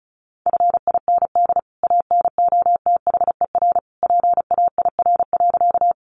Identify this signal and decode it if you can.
Morse code, heard as a keyed tone, about 34 wpm, 700 Hz. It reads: FIND ANOTHER PAIR.
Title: Sentences from Top 1200 Words 35wpm